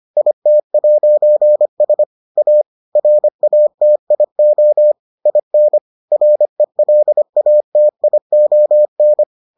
Morse code, heard as a keyed tone, about 25 wpm, 600 Hz. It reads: IT'S A RATIO IN RELATION